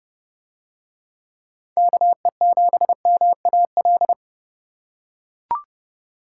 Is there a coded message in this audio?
Yes: Morse code from 1.8 to 4.1 s, heard as a keyed tone, about 30 wpm, 700 Hz. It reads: KE7MAL